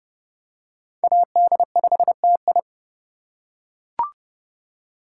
Morse code, heard as a keyed tone, about 26 wpm, 700 Hz. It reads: AD5TI